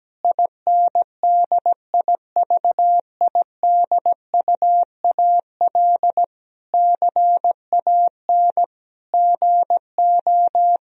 Morse code, heard as a keyed tone, about 17 wpm, 700 Hz. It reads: INDIVIDUAL CAN GO
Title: Sentences from Top 1000 Words - Review Part 4 - 17wpm